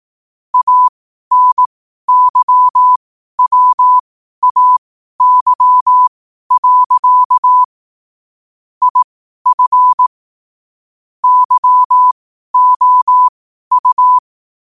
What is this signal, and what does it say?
Morse code, heard as a keyed tone, about 18 wpm, spaced as for 12 wpm, 1000 Hz. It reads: ANYWAY. IF YOU
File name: qotdfiles / 12wpm